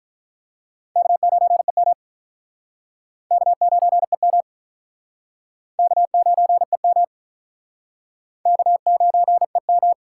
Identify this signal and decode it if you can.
Morse code, heard as a keyed tone, about 47 wpm, 700 Hz. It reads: K9EM K9EM K9EM K9EM